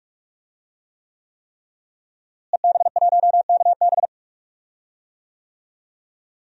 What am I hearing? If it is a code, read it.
Morse code, heard as a keyed tone, about 46 wpm, 700 Hz. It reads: EB1KB
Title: DXCC-100 Call Signs - 5 Characters - Repeated 3x - 45wpm